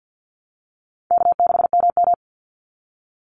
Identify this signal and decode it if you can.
Morse code, heard as a keyed tone, about 50 wpm, 700 Hz. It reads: K6MM
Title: Easy Call Signs - 50wpm